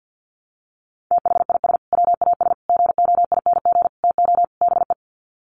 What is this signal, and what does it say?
Morse code, heard as a keyed tone, about 50 wpm, 700 Hz. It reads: THIS WAS GOING TO BE
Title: Sets of 5 Words Encore2 50wpm